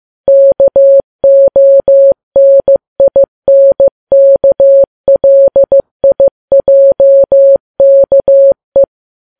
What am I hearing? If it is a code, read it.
Morse code, heard as a keyed tone, about 15 wpm, 550 Hz. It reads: KONINKLIJKE
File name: Files Web tls / Morse on9fv / morse15kps